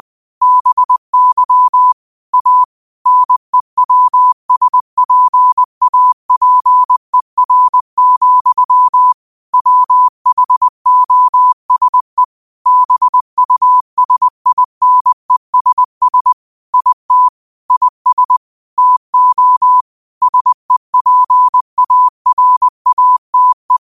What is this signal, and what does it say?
Morse code, heard as a keyed tone, about 20 wpm, 1000 Hz. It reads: BY A NEWSPAPER, WHOSE BUSINESS IT IS TO SEPARATE
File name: qotdfiles / 20wpm